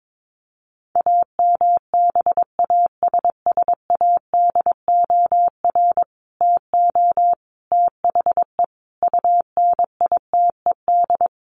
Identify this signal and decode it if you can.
Morse code, heard as a keyed tone, about 22 wpm, 700 Hz. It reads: AMBASSADOR TO THE UNITED
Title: Sets of 5 Words Encore2 22wpm